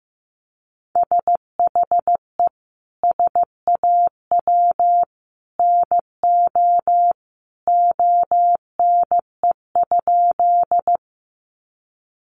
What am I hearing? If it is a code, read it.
Morse code, heard as a keyed tone, about 14 wpm, 700 Hz. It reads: SHE SAW NO ONE?
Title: Sentences from Top 300 Words 15wpm